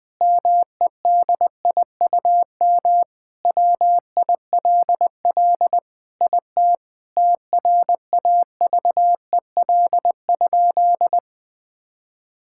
Morse code, heard as a keyed tone, about 21 wpm, 700 Hz. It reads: MEDIUM WILL IT TRAVEL?
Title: Sentences from Top 1000 Words - Review Part 6 - 20wpm